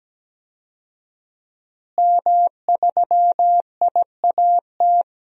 Morse code, heard as a keyed tone, about 17 wpm, 700 Hz. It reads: M3IAT
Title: DXCC-100 Call Signs - 5 Characters - Repeated 2x - 17wpm